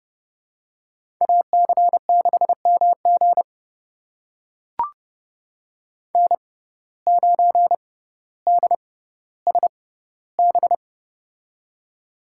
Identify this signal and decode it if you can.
Morse code, heard as a keyed tone, about 29 wpm, spaced as for 15 wpm, 700 Hz. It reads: AC6MG N9DSB
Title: USA Call Signs - 5 Characters - 6x Character Spacing - 30wpm